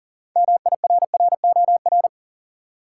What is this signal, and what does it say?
Morse code, heard as a keyed tone, about 39 wpm, 700 Hz. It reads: MIRROR
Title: Six Letter Words - 40wpm